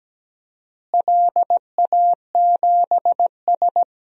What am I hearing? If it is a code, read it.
Morse code, heard as a keyed tone, about 17 wpm, 700 Hz. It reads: LA7S
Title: DXCC-100 Call Signs - 4 Characters - Repeated 3x - 17wpm